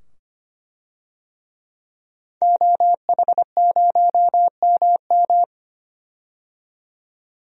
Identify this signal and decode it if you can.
Morse code, heard as a keyed tone, about 25 wpm, 700 Hz. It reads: OH0MM
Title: DXCC-100 Call Signs - 5 Characters - 25wpm